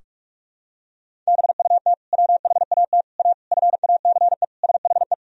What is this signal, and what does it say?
Morse code, heard as a keyed tone, about 46 wpm, 700 Hz. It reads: BUT WHAT A FACE SHE
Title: Sentences from Top 1000 Words - Review Part 2 - 45wpm